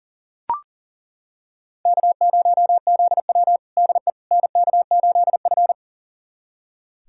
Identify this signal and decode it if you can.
Morse code, heard as a keyed tone, about 40 wpm, 700 Hz. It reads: K0ZW DE NK8F
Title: USA Callsign DE Callsign - 4 Characters - 40wpm